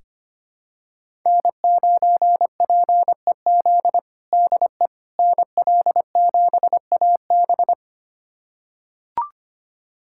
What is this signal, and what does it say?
Morse code, heard as a keyed tone, about 25 wpm, 700 Hz. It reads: N9PEZ DE NL7AB
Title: USA Callsign DE Callsign - 5 Characters - 25wpm